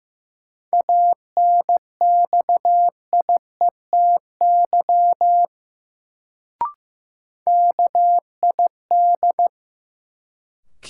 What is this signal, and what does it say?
Morse code, heard as a keyed tone, about 15 wpm, 700 Hz. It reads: ANXIETY KID
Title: Sentences from Top 1600 Words - New Word Drill Alt - 15wpm